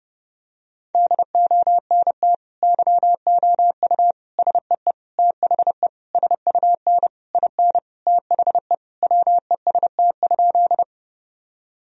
Morse code, heard as a keyed tone, about 29 wpm, 700 Hz. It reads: DONT YOU SEE THE SUN IN THE WEST?
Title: Sentences from Top 1000 Words - Review Part 2 - 30wpm